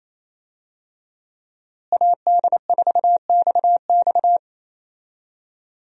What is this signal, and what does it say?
Morse code, heard as a keyed tone, about 28 wpm, 700 Hz. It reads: AD4XX
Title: USA Call Signs - 5 Characters - Repeated 3x - 28wpm